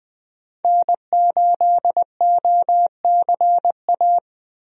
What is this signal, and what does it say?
Morse code, heard as a keyed tone, about 20 wpm, 700 Hz. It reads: N8OCA
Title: USA Call Signs - 5 Characters - Repeated 3x - 20wpm